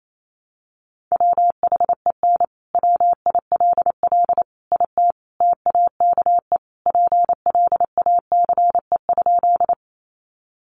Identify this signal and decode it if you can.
Morse code, heard as a keyed tone, about 28 wpm, 700 Hz. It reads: WHEN WILL IT TAKE PLACE?
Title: Sentences from Top 200 Words 28wpm